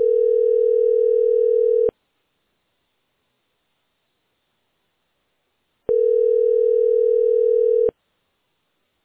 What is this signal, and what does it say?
Ringback tone (North America), 440 + 480 Hz.